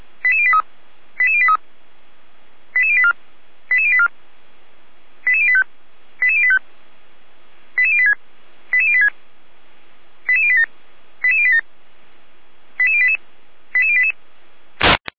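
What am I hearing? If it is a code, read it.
Five-tone selective call (ZVEI), heard as quick runs of five tones, tones of about 70 ms. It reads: 88083 88083 88084 88084 88085 88085 88086 88086 88087 88087 88088 88088
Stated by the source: ZVEI five-tone call received by SDR.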